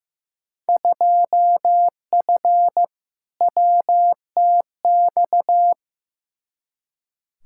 Morse code, heard as a keyed tone, about 15 wpm, 700 Hz. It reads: 2F WTX
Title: Field Day Exchanges - 15wpm